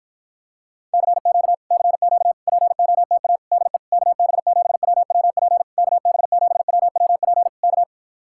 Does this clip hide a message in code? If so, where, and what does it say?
Yes: Morse code from 0.9 to 7.8 s, heard as a keyed tone, about 54 wpm, 700 Hz. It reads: CQ CQ POTA DE KB7WWP KB7WWP K